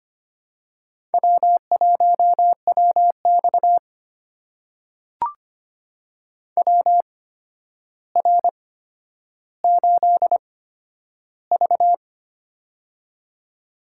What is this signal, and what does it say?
Morse code, heard as a keyed tone, about 25 wpm, spaced as for 11 wpm, 700 Hz. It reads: W1WX WR8V
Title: USA Call Signs - 4 Characters - 8x Character Spacing - 25wpm